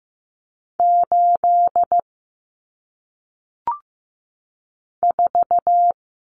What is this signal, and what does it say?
Morse code, heard as a keyed tone, about 15 wpm, 700 Hz. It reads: 8 4
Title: Numbers - 1-Digit 15wpm